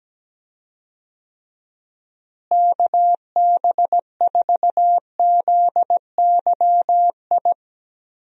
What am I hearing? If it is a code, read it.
Morse code, heard as a keyed tone, about 17 wpm, 700 Hz. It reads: KB4ZYI
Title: USA Call Signs - 6 Characters - Repeated 2x - 17wpm